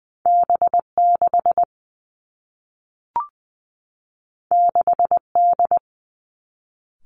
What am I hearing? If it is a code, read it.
Morse code, heard as a keyed tone, about 20 wpm, 700 Hz. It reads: B 6 6 D
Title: Two Challenging Letter-Number Combinations 20wpm